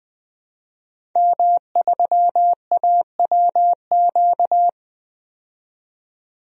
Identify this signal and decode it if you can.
Morse code, heard as a keyed tone, about 20 wpm, 700 Hz. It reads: M3AWQ
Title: DXCC-100 Call Signs - 5 Characters - 20wpm